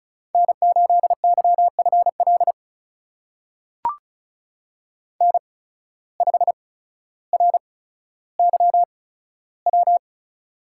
Morse code, heard as a keyed tone, about 36 wpm, spaced as for 15 wpm, 700 Hz. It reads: N8YFL N5RYW
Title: USA Call Signs - 8x Character Spacing - 35wpm